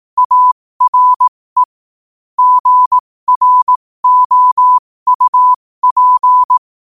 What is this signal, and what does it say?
Morse code, heard as a keyed tone, about 18 wpm, spaced as for 15 wpm, 1000 Hz. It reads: ARE GROUP
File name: qotdfiles / 15wpm